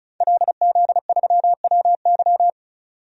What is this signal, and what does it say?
Morse code, heard as a keyed tone, about 34 wpm, 700 Hz. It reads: LZ3WY
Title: DXCC-100 Call Signs - 5 Characters - 35wpm